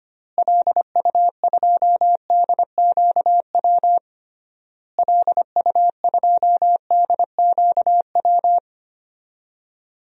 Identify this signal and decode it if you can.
Morse code, heard as a keyed tone, about 25 wpm, 700 Hz. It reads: LU2DQW LU2DQW
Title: DXCC-100 Call Signs - 6 Characters - Repeated 3x - 25wpm